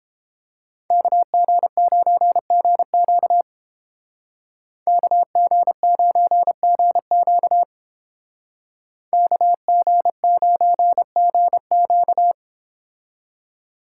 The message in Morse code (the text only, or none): KG9GQ KG9GQ KG9GQ